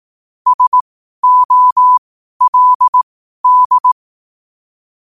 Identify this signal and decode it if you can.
Morse code, heard as a keyed tone, about 18 wpm, spaced as for 12 wpm, 1000 Hz. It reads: SOLD
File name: qotdfiles / 12wpm